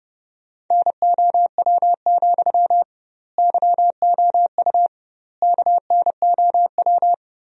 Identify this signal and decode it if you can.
Morse code, heard as a keyed tone, about 31 wpm, 700 Hz. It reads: NOW, YOU KNOW
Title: 2x Word-Spacing - Sentences from Top 200 Words - 30wpm